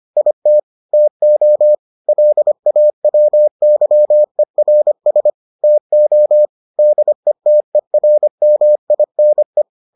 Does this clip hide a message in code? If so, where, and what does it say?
Yes: Morse code from 0.2 to 9.6 s, heard as a keyed tone, about 25 wpm, 600 Hz. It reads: IT TO LAWYERS TO DETERMINE